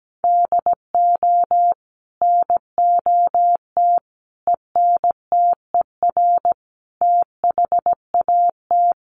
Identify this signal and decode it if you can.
Morse code, heard as a keyed tone, about 17 wpm, 700 Hz. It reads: DO NOT ENTER THAT
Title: Sentences from Top 1300 Words 17wpm